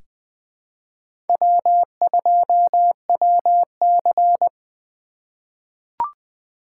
Morse code, heard as a keyed tone, about 20 wpm, 700 Hz. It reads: W2WC
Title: USA Call Signs - 4 Characters - 3x Character Spacing - 20wpm